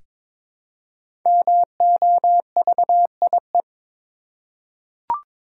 Morse code, heard as a keyed tone, about 22 wpm, 700 Hz. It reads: MOVIE
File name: 5 Letter Words - First 3 Letters Given - 22wpm